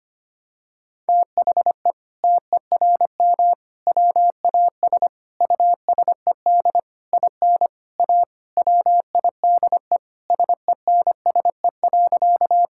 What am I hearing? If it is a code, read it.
Morse code, heard as a keyed tone, about 25 wpm, 700 Hz. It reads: THE TERM WAS USED IN A WIDE SENSE.
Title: Sentences from Top 1000 Words - Review Part 6 - 25wpm